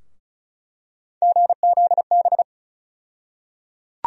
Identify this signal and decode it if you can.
Morse code, heard as a keyed tone, about 36 wpm, 700 Hz. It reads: GZB
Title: USA Call Sign Postfixes - 3 Letters 35wpm